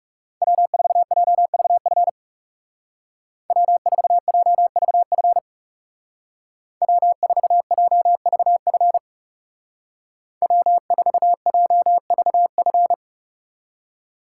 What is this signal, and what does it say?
Morse code, heard as a keyed tone, about 36 wpm, 700 Hz. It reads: W4JVF W4JVF W4JVF W4JVF
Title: Speed Racing - USA Call Signs 30wpm